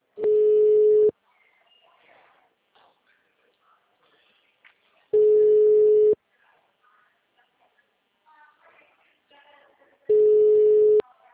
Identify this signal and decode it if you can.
Ringback tone (Europe, 425 Hz), with speech.